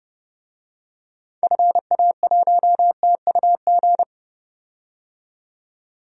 Morse code, heard as a keyed tone, about 30 wpm, 700 Hz. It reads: FA1TUG
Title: DXCC-100 Call Signs - 6 Characters - 30wpm